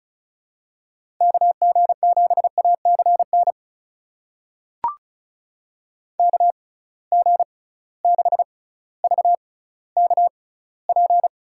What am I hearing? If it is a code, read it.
Morse code, heard as a keyed tone, about 36 wpm, spaced as for 18 wpm, 700 Hz. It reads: KG7ACN KG6VKP
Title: USA Call Signs - 6 Characters - 6x Character Spacing - 35wpm